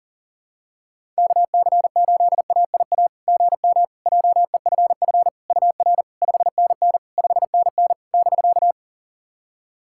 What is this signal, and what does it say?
Morse code, heard as a keyed tone, about 40 wpm, 700 Hz. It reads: KC8AIA GM JEFF UR 5NN 5NN <BK>
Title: SOTA QSO Practice 40wpm